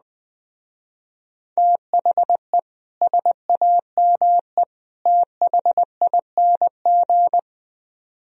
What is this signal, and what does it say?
Morse code, heard as a keyed tone, about 20 wpm, 700 Hz. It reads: THE SAME THING